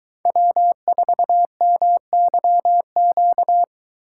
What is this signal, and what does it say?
Morse code, heard as a keyed tone, about 22 wpm, 700 Hz. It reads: W4MYQ